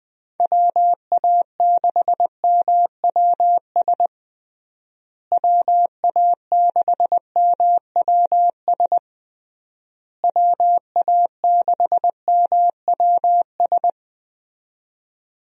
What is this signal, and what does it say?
Morse code, heard as a keyed tone, about 20 wpm, 700 Hz. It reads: WA6MWS WA6MWS WA6MWS
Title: USA Call Signs - 6 Characters - Repeated 3x - 20wpm